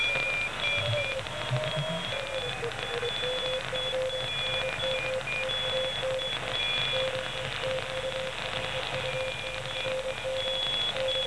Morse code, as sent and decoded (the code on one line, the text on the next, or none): none